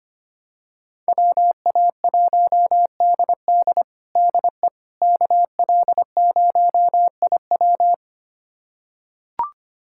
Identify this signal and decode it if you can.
Morse code, heard as a keyed tone, about 25 wpm, 700 Hz. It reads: WA1DD DE KL0IW
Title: USA Callsign DE Callsign - 5 Characters - 25wpm